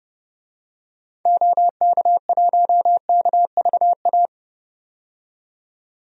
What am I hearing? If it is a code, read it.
Morse code, heard as a keyed tone, about 30 wpm, 700 Hz. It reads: OK1KVA
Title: DXCC-100 Call Signs - 6 Characters - 30wpm